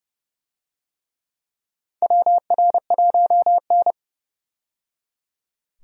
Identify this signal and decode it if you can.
Morse code, heard as a keyed tone, about 32 wpm, 700 Hz. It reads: WR1N
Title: USA Call Signs - 4 Characters - 30wpm